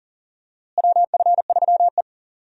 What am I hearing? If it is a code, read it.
Morse code, heard as a keyed tone, about 40 wpm, 700 Hz. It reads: WF3E
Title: USA Call Signs - 4 Characters - Repeated 2x - 40wpm